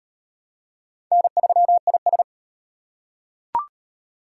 Morse code, heard as a keyed tone, about 35 wpm, 700 Hz. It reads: N3IS